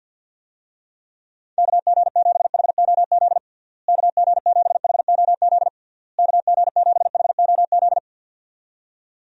Morse code, heard as a keyed tone, about 52 wpm, 700 Hz. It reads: KG7HOZ KG7HOZ KG7HOZ
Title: USA Call Signs - 6 Characters - Repeated 3x - 50wpm